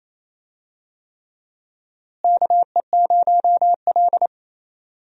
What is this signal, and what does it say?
Morse code, heard as a keyed tone, about 28 wpm, 700 Hz. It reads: KE0L
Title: USA Call Signs - 4 Characters - Repeated 3x - 28wpm